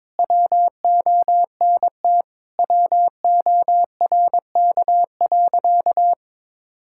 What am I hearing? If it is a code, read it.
Morse code, heard as a keyed tone, about 22 wpm, 700 Hz. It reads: WONT WORK.
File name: practice / 22wpm / Sentences with 3-Word Phrases - 22wpm